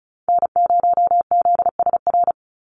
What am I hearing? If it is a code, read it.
Morse code, heard as a keyed tone, about 34 wpm, 700 Hz. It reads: N0ZSR
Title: USA Call Signs 35wpm